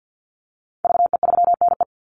Morse code, heard as a keyed tone, about 50 wpm, 700 Hz. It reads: VE3NE
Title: Call Signs Repeated 2x - 50wpm